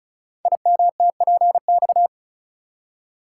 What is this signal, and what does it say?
Morse code, heard as a keyed tone, about 36 wpm, 700 Hz. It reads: IMTPX
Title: Code Groups - 5 Letters - 35wpm